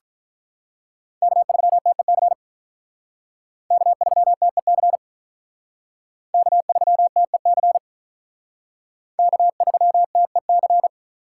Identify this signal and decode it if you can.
Morse code, heard as a keyed tone, about 44 wpm, 700 Hz. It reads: K3TEC K3TEC K3TEC K3TEC